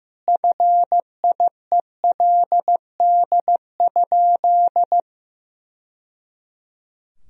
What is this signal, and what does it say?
Morse code, heard as a keyed tone, about 15 wpm, 700 Hz. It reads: FIELD?
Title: Sentences from Top 1000 Words - Review Part 2 - 15wpm